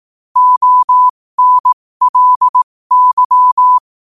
Morse code, heard as a keyed tone, about 18 wpm, spaced as for 15 wpm, 1000 Hz. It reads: ONLY
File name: qotdfiles / 15wpm